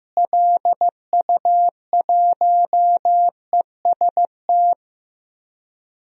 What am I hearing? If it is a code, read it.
Morse code, heard as a keyed tone, about 15 wpm, 700 Hz. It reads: LU1EST
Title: DXCC-100 Call Signs - 6 Characters - Repeated 3x - 15wpm